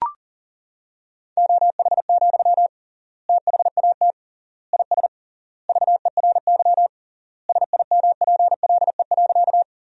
Morse code, heard as a keyed tone, about 40 wpm, 700 Hz. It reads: OH, THAT IS VERY SIMPLE.
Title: 3x Word-Spacing - Sentences from Top 500 Words - 40wpm